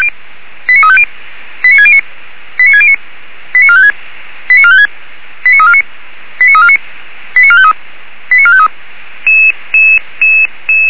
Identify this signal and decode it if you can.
Five-tone selective call (ZVEI), heard as quick runs of five tones, tones of about 70 ms. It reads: 89350 89609 89609 89456 89456 89349 89349 89453 89453